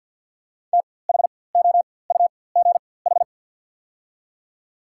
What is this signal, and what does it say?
Morse code, heard as a keyed tone, about 54 wpm, spaced as for 24 wpm, 700 Hz. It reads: THOUGH
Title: 4x Character Spacing - Top 500 Words - 50wpm